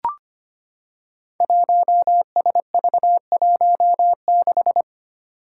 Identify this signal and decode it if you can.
Morse code, heard as a keyed tone, about 25 wpm, 700 Hz. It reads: 1SV16
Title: Five Challenging Letter-Number Combinations 25wpm